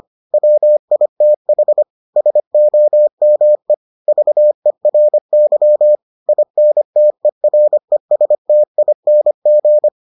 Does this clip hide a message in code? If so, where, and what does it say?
Yes: Morse code from 0.3 to 9.9 s, heard as a keyed tone, about 25 wpm, 600 Hz. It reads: WITH SOME VERY INTERESTING